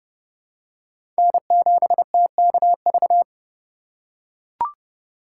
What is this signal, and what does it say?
Morse code, heard as a keyed tone, about 32 wpm, 700 Hz. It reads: N7TKV
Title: USA Call Signs - 5 Characters - 3x Character Spacing - 30wpm